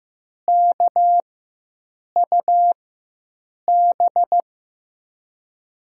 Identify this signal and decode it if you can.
Morse code, heard as a keyed tone, about 15 wpm, spaced as for 7 wpm, 700 Hz. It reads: KUB
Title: USA Call Signs - 6 Characters - 4x Character Spacing - 15wpm